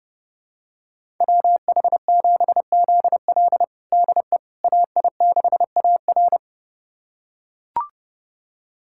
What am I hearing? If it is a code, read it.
Morse code, heard as a keyed tone, about 30 wpm, 700 Hz. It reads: WH7ZL DE AI6AR
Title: USA Callsign DE Callsign - 5 Characters - 30wpm